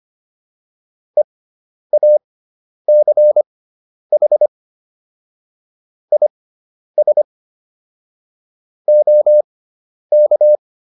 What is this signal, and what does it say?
Morse code, heard as a keyed tone, about 25 wpm, spaced as for 10 wpm, 600 Hz. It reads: EACH IS OK